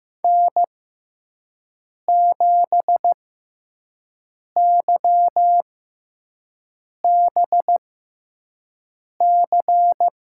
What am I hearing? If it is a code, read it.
Morse code, heard as a keyed tone, about 15 wpm, spaced as for 5 wpm, 700 Hz. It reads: N7YBC
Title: USA Call Signs - 5 Characters - 6x Character Spacing - 15wpm